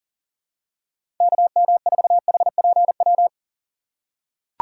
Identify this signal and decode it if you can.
Morse code, heard as a keyed tone, about 40 wpm, 700 Hz. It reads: KM4HPW
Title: USA Call Signs - 6 Characters - 3x Character Spacing - 40wpm